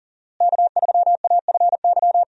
Morse code, heard as a keyed tone, about 40 wpm, 700 Hz. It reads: K3AFY
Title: USA Call Signs - 5 Characters - Repeated 2x - 40wpm